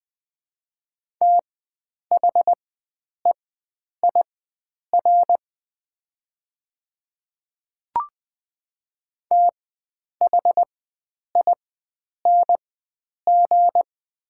Morse code, heard as a keyed tone, about 20 wpm, spaced as for 9 wpm, 700 Hz. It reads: THEIR THING
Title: 4x Character Spacing - Top 100 Words - 20wpm